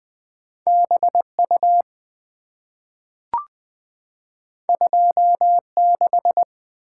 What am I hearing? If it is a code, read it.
Morse code, heard as a keyed tone, about 20 wpm, 690 Hz. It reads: BU 26
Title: Two Letter-Number Combinations - 20wpm